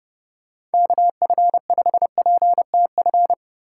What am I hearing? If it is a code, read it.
Morse code, heard as a keyed tone, about 31 wpm, 700 Hz. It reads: KF5PTF